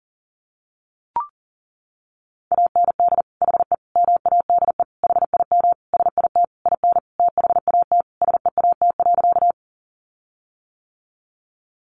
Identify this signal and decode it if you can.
Morse code, heard as a keyed tone, about 40 wpm, 700 Hz. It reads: AND HE MADE HIM SIT IN THAT SEAT.
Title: Sentences from Top 1500 Words 40wpm